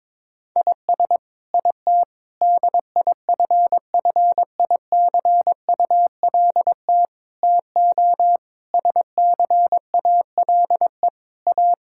Morse code, heard as a keyed tone, about 22 wpm, 700 Hz. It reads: IS IT DIFFICULT TO SCALE A
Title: Sentences from Top 1000 Words - Review Part 5 - 22wpm